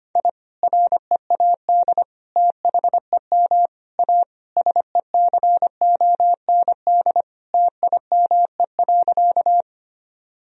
Morse code, heard as a keyed tone, about 25 wpm, 700 Hz. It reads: I READ THEM A SECOND TIME.
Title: Sentences from Top 1000 Words - Review Part 1 - 25wpm